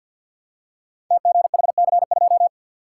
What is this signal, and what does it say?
Morse code, heard as a keyed tone, about 47 wpm, 700 Hz. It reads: TGHC2
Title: Code Groups - 5 Letter:Numbers - 50wpm